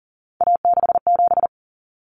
Morse code, heard as a keyed tone, about 38 wpm, 700 Hz. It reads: A67
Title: Common Cut Numbers - 3-Digit 40wpm